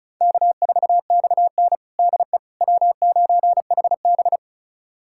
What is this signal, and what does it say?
Morse code, heard as a keyed tone, about 36 wpm, 700 Hz. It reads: K4XN DE W9HB
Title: USA Callsign DE Callsign - 4 Characters - 35wpm